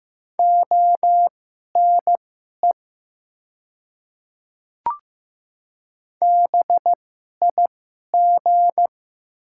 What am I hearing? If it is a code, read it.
Morse code, heard as a keyed tone, about 15 wpm, spaced as for 11 wpm, 700 Hz. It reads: ONE BIG
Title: 2x Character Spacing - Top 200 Words - 15wpm